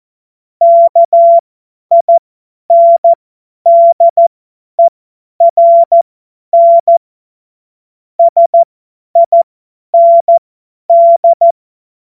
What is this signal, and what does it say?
Morse code, heard as a keyed tone, about 14 wpm, spaced as for 10 wpm, 680 Hz. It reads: KINDERN SIND
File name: Morsekurs / Witze / Witz 01. November 2022 50 F70